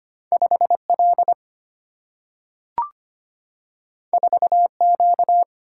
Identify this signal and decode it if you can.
Morse code, heard as a keyed tone, about 25 wpm, 700 Hz. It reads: 5L 4Q
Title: Single-Letter Single-Number 25wpm